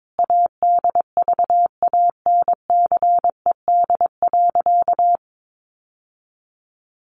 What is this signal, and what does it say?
Morse code, heard as a keyed tone, about 22 wpm, 700 Hz. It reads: ADVANCED.
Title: Sentences from Top 1200 Words 22wpm